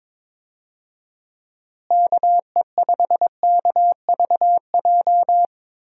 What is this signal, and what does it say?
Morse code, heard as a keyed tone, about 22 wpm, 700 Hz. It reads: KE5KVJ
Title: USA Call Signs - 6 Characters - Repeated 3x - 22wpm